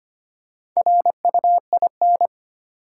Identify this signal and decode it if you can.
Morse code, heard as a keyed tone, about 25 wpm, 700 Hz. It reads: RUIN
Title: Four Letter Words - 25wpm